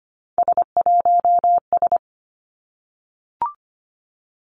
Morse code, heard as a keyed tone, about 25 wpm, 700 Hz. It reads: S1S
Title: Three Challenging Letter-Number Combinations 25wpm